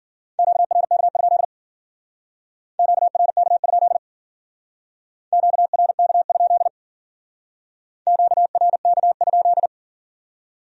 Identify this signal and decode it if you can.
Morse code, heard as a keyed tone, about 50 wpm, 700 Hz. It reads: QRK? QRK? QRK? QRK?